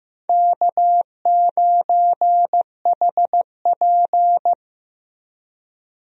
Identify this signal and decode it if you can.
Morse code, heard as a keyed tone, about 16 wpm, 700 Hz. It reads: K9HP
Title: USA Call Signs - 4 Characters - Repeated 2x - 15wpm